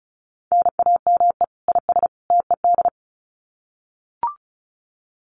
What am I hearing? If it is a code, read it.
Morse code, heard as a keyed tone, about 34 wpm, 700 Hz. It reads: NAME IS TED
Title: Common QSO Phrases 35wpm